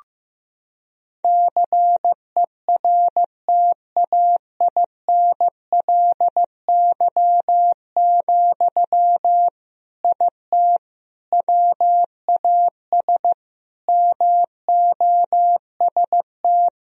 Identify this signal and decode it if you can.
Morse code, heard as a keyed tone, about 15 wpm, 700 Hz. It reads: CERTAINLY, IT WAS MOST
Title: Sentences from Top 1000 Words - Review Part 1 - 15wpm